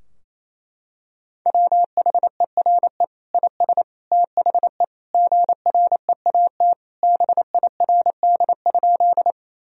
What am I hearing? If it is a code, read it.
Morse code, heard as a keyed tone, about 28 wpm, 700 Hz. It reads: WHERE IS THE GREAT BIRD?
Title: Sentences from Top 1000 Words - Review Part 2 - 28wpm